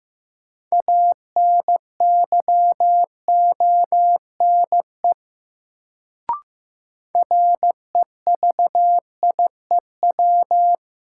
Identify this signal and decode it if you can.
Morse code, heard as a keyed tone, about 15 wpm, 700 Hz. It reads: ANYONE REVIEW
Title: Sentences from Top 900 Words - New Word Drill - 15wpm